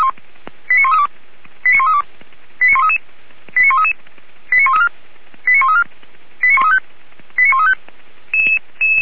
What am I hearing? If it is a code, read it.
Five-tone selective call (ZVEI), heard as quick runs of five tones, tones of about 70 ms. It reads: 89132 89132 89133 89133 89134 89134 89135 89135
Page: /464/SDR/ZVEI/89115/2022.03.05/